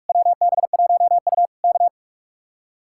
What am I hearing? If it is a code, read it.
Morse code, heard as a keyed tone, about 66 wpm, spaced as for 55 wpm, 700 Hz. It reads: WB1U K
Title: Speed-Racing Common QSO Phrases - 45wpm